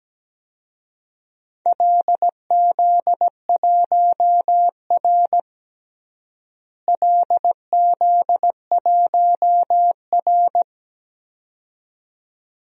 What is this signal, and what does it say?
Morse code, heard as a keyed tone, about 17 wpm, 700 Hz. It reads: LZ1R LZ1R
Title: DXCC-100 Call Signs - 4 Characters - Repeated 2x - 17wpm